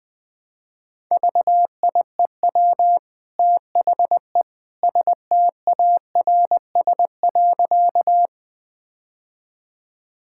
Morse code, heard as a keyed tone, about 20 wpm, 700 Hz. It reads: VIEW THE STARS.
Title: Easy Sentences with Hints - 20wpm